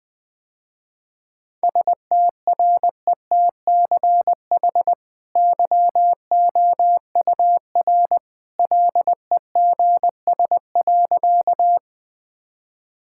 Morse code, heard as a keyed tone, about 20 wpm, 700 Hz. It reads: STRETCH YOUR LEGS.
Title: Easy Sentences with Hints - 20wpm